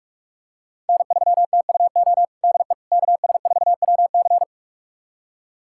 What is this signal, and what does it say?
Morse code, heard as a keyed tone, about 41 wpm, 700 Hz. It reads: N3TUO DE KS4WC
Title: USA Callsign DE Callsign - 5 Characters - 45wpm